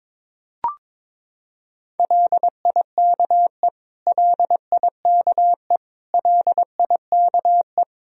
Morse code, heard as a keyed tone, about 22 wpm, 700 Hz. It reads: LIKE LIKE LIKE
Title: Top 100 Words Repeated 22wpm